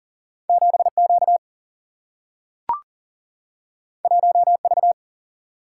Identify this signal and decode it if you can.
Morse code, heard as a keyed tone, about 39 wpm, 700 Hz. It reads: 7Q 1V
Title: Single-Letter Single-Number 40wpm